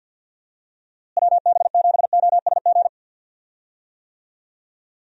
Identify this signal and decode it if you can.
Morse code, heard as a keyed tone, about 51 wpm, 700 Hz. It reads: WB7OSG
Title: USA Call Signs - 6 Characters - Repeated 3x - 50wpm